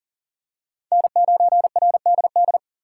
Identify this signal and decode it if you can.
Morse code, heard as a keyed tone, about 40 wpm, 690 Hz. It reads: N9RDD
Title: USA Call Signs - 5 Characters - 3x Character Spacing - 40wpm